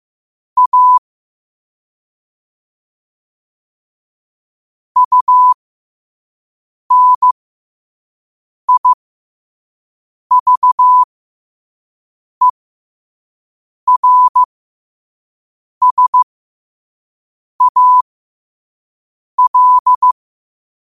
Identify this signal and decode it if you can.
Morse code, heard as a keyed tone, about 15 wpm, spaced as for 5 wpm, 1000 Hz. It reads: A UNIVERSAL
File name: qotdfiles / 5wpm